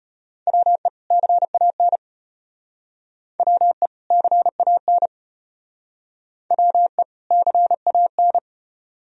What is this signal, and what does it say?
Morse code, heard as a keyed tone, about 32 wpm, 700 Hz. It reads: WE CAN WE CAN WE CAN